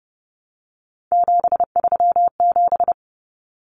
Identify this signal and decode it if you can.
Morse code, heard as a keyed tone, about 30 wpm, 700 Hz. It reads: 737
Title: Common Cut Numbers - 3-Digit 30wpm